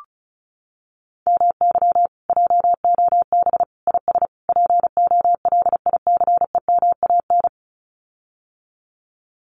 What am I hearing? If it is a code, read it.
Morse code, heard as a keyed tone, about 34 wpm, 700 Hz. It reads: MY JOB IS POLICEMAN
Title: Common QSO Phrases Encore 35wpm